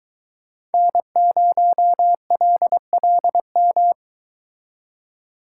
Morse code, heard as a keyed tone, about 23 wpm, 700 Hz. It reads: N0LLM